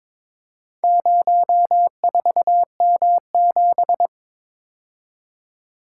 Morse code, heard as a keyed tone, about 22 wpm, 700 Hz. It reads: 04M7
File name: practice / 22wpm / Four Letter-Number Combinations - 22wpm